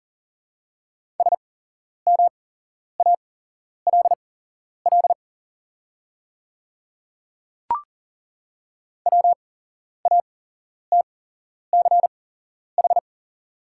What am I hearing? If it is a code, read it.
Morse code, heard as a keyed tone, about 40 wpm, spaced as for 11 wpm, 690 Hz. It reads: SMALL WATCH
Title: 8x Character Spacing - Top 500 Words - 40wpm